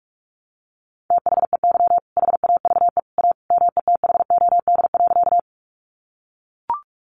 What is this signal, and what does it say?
Morse code, heard as a keyed tone, about 44 wpm, 700 Hz. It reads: THEY HAVE A METHOD.
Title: Sentences from Top 600 Words 45wpm